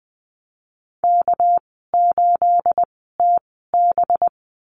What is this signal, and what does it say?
Morse code, heard as a keyed tone, about 21 wpm, spaced as for 15 wpm, 700 Hz. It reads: K8TB
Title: 238 Lesson 38g - Letter q - Call Signs 20wpm